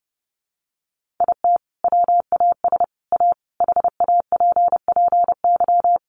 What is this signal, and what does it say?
Morse code, heard as a keyed tone, about 30 wpm, 700 Hz. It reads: IT WAS A HAPPY
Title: Sentences from Top 1000 Words 30wpm